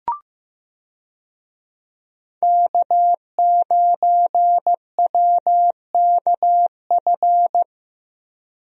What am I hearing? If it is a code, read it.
Morse code, heard as a keyed tone, about 15 wpm, 700 Hz. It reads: K9WKF
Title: USA Call Signs - 5 Characters - 15wpm